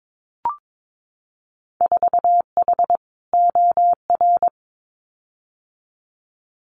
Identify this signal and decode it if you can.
Morse code, heard as a keyed tone, about 22 wpm, 700 Hz. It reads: 4H OR
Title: Winter Field Day Exchanges Encore 22wpm